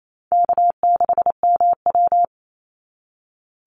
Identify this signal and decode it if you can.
Morse code, heard as a keyed tone, about 28 wpm, 700 Hz. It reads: K6MW
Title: NA SOTA Chaser Callsigns 28wpm